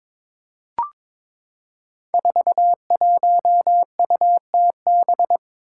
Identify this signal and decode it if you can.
Morse code, heard as a keyed tone, about 22 wpm, 700 Hz. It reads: 41UTB
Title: Five Challenging Letter-Number Combinations 22wpm